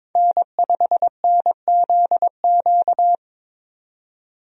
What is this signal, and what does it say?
Morse code, heard as a keyed tone, about 22 wpm, 700 Hz. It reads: N5NZQ
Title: USA Call Signs - 5 Characters - 22wpm